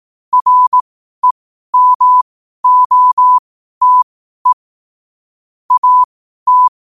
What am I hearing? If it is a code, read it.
Morse code, heard as a keyed tone, about 18 wpm, spaced as for 12 wpm, 1000 Hz. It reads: REMOTE AT
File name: qotdfiles / 12wpm